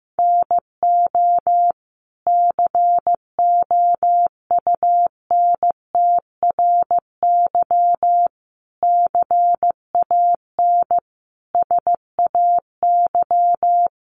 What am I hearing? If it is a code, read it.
Morse code, heard as a keyed tone, about 15 wpm, 700 Hz. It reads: NO COUNTRY CAN SAY
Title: Sentences from Top 800 Words 15wpm